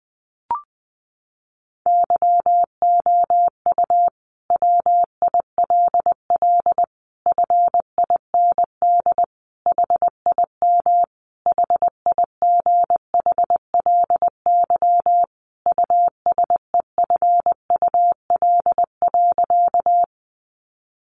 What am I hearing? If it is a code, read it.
Morse code, heard as a keyed tone, about 20 wpm, 700 Hz. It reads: YOU WILL FIND HIM HIGHLY USEFUL.
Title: Sentences from Top 1100 Words 20wpm